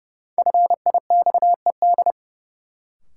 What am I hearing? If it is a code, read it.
Morse code, heard as a keyed tone, about 30 wpm, 700 Hz. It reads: FIXED
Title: Sentences from Top 1100 Words - New Word Drill - 30wpm